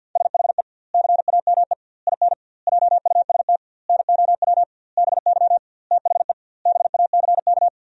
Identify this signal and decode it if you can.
Morse code, heard as a keyed tone, about 50 wpm, 700 Hz. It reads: SHE CAME IN JUST NOW BY THE BACK
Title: 2x Word-Spacing - Sentences from Top 300 Words - 50wpm